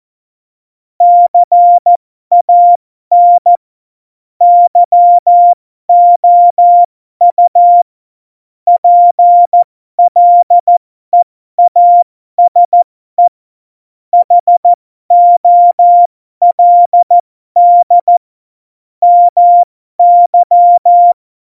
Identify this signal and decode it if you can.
Morse code, heard as a keyed tone, about 14 wpm, spaced as for 12 wpm, 700 Hz. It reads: CAN YOU PLEASE HOLD MY
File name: Morsekurs / Witze